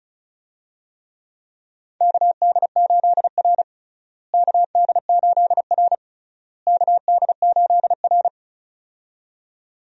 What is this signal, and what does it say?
Morse code, heard as a keyed tone, about 36 wpm, 700 Hz. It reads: KD8R KD8R KD8R
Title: USA Call Signs - 4 Characters - Repeated 3x - 35wpm